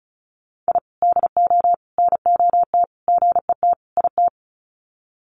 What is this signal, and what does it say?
Morse code, heard as a keyed tone, about 34 wpm, 700 Hz. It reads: I DO NOT GET IT
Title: Sets of 5 Words Encore 35wpm